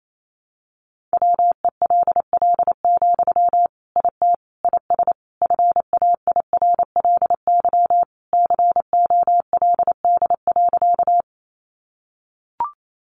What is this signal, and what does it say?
Morse code, heard as a keyed tone, about 28 wpm, 700 Hz. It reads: WELL, IT IS FAIRLY COLD.
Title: Sentences from Top 1100 Words 28wpm